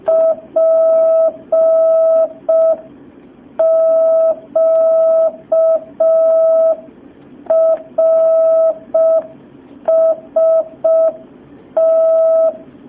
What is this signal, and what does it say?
Morse code, heard as a keyed tone, about 5 wpm, 650 Hz. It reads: PQRST
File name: jh8cbh / 2019cw